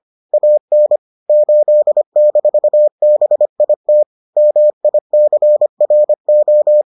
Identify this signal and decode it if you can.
Morse code, heard as a keyed tone, about 25 wpm, 600 Hz. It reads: AN 8-BIT MICRO